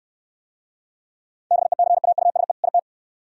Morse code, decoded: BECAUSE IT

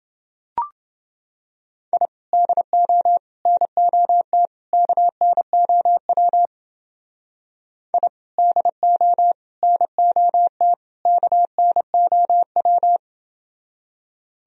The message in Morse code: I DO NOT KNOW I DO NOT KNOW

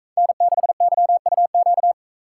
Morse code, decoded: N6YUQ

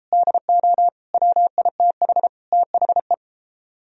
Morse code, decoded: DO WITH THE

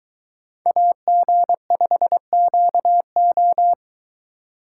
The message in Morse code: AG5QO

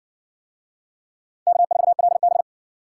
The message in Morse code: K4LB

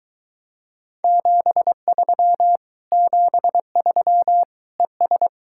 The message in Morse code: 73 73 ES

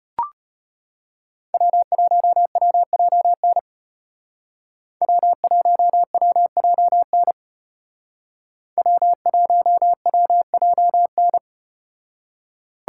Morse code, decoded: W1WJN W1WJN W1WJN

